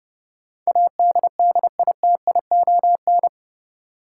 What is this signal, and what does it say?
Morse code, heard as a keyed tone, about 30 wpm, 700 Hz. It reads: ADDITION
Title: Sentences from Top 1500 Words - New Word Drill Alt - 30wpm